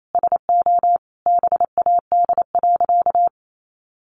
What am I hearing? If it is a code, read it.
Morse code, heard as a keyed tone, about 28 wpm, 700 Hz. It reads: SO BAD.
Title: Sentences from Top 700 Words 28wpm